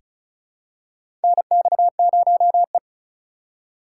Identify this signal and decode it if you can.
Morse code, heard as a keyed tone, about 34 wpm, 700 Hz. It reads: NX0E
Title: USA Call Signs - 4 Characters - 4x Character Spacing - 35wpm